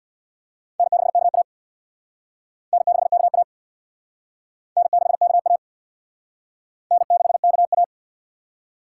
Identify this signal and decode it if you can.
Morse code, heard as a keyed tone, about 62 wpm, 700 Hz. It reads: N6KA N6KA N6KA N6KA
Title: Speed Racing - USA 1x2 Call Signs 50wpm